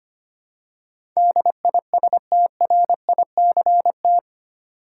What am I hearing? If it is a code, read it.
Morse code, heard as a keyed tone, about 25 wpm, 700 Hz. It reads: DISTRICT
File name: Eight Letter Words - 25wpm